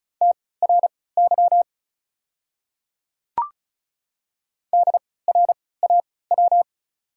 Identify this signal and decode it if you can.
Morse code, heard as a keyed tone, about 34 wpm, spaced as for 20 wpm, 700 Hz. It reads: TRY DRAW
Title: 3x Character Spacing - Top 500 Words - 35wpm